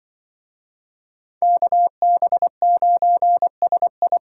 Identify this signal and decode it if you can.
Morse code, heard as a keyed tone, about 25 wpm, 700 Hz. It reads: KB9SI
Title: Speed Racing - USA 2x2 Call Signs 20wpm